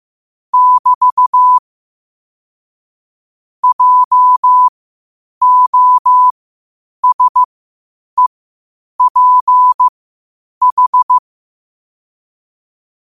Morse code, heard as a keyed tone, about 15 wpm, spaced as for 8 wpm, 1000 Hz. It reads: <BT> JOSEPH
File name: qotdfiles / 8wpm